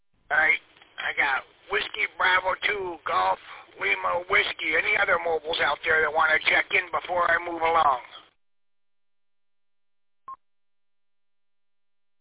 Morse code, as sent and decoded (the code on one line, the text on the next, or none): none